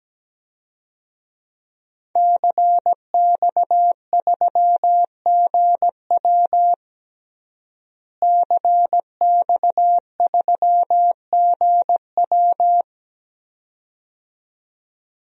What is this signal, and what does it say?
Morse code, heard as a keyed tone, about 17 wpm, 700 Hz. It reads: CX3GW CX3GW
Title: DXCC-100 Call Signs - 5 Characters - Repeated 2x - 17wpm